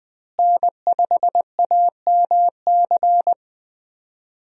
Morse code, heard as a keyed tone, about 20 wpm, 700 Hz. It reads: N5AMC
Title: USA Call Signs - 5 Characters - Repeated 2x - 20wpm